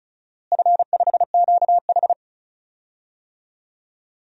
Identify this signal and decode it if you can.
Morse code, heard as a keyed tone, about 36 wpm, 700 Hz. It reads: F5QH
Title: DXCC-100 Call Signs - 4 Characters - Repeated 3x - 35wpm